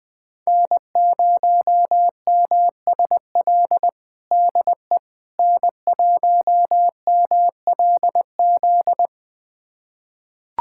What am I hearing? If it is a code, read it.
Morse code, heard as a keyed tone, about 20 wpm, 700 Hz. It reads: N0MSL DE N1MLZ
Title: USA Callsign DE Callsign - 5 Characters - 20wpm